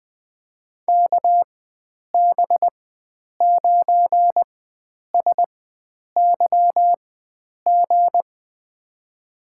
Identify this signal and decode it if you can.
Morse code, heard as a keyed tone, about 20 wpm, spaced as for 9 wpm, 700 Hz. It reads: KB9SYG
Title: USA Call Signs - 6 Characters - 4x Character Spacing - 20wpm